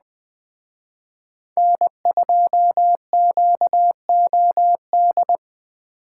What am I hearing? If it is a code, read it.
Morse code, heard as a keyed tone, about 20 wpm, 700 Hz. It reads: N2QOD